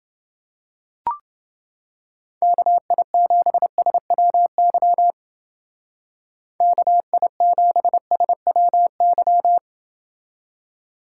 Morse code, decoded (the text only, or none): KI7SWY KI7SWY